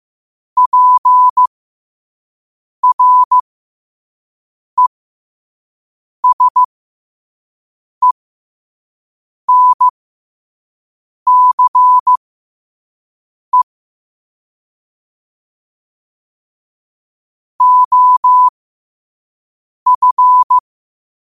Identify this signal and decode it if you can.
Morse code, heard as a keyed tone, about 15 wpm, spaced as for 5 wpm, 1000 Hz. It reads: PRESENCE OF
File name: qotdfiles / 5wpm